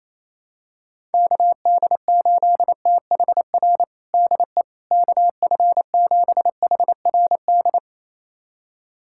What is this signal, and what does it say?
Morse code, heard as a keyed tone, about 28 wpm, 700 Hz. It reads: KD8THR DE KF7HRD
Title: USA Callsign DE Callsign - 6 Characters - 28wpm